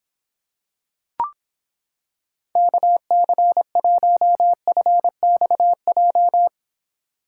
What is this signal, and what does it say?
Morse code, heard as a keyed tone, about 26 wpm, 700 Hz. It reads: KC1FXJ